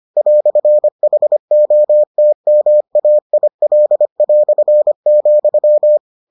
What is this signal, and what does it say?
Morse code, heard as a keyed tone, about 25 wpm, 600 Hz. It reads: "HOTMAIL",